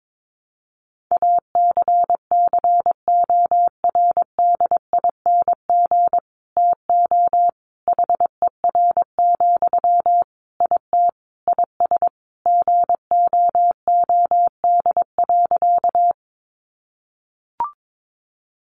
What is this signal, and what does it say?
Morse code, heard as a keyed tone, about 22 wpm, 700 Hz. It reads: ACCORDING TO HER, IT IS GOOD.
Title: Sentences from Top 800 Words 22wpm